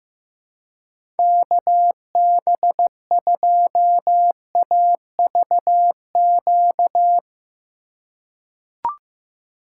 Morse code, heard as a keyed tone, about 15 wpm, 700 Hz. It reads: KB2AVQ
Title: USA Call Signs - 6 Characters - 2x Character Spacing - 15wpm